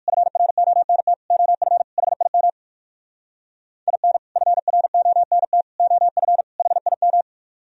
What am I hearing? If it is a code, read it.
Morse code, heard as a keyed tone, about 65 wpm, spaced as for 56 wpm, 700 Hz. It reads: FRONT OF HIM IN FRONT OF HIM